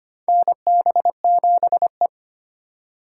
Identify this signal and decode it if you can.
Morse code, heard as a keyed tone, about 25 wpm, 700 Hz. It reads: NB7E
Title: USA Call Signs - 4 Characters - 4x Character Spacing - 25wpm